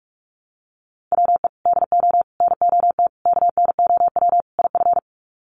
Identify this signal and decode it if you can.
Morse code, heard as a keyed tone, about 44 wpm, 700 Hz. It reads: WE DO NOT KNOW IF
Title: Sets of 5 Words Encore2 45wpm